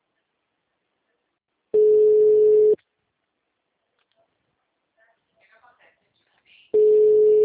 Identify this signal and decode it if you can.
Ringback tone (Europe, 425 Hz), with speech.